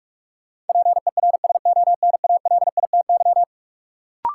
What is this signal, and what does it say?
Morse code, heard as a keyed tone, about 44 wpm, 700 Hz. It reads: PERSONALITY
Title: Sentences from Top 1400 Words - New Word Drill Alt - 45wpm